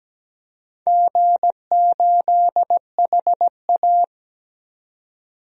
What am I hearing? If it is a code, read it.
Morse code, heard as a keyed tone, about 17 wpm, 700 Hz. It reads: G8HA
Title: DXCC-100 Call Signs - 4 Characters - 17wpm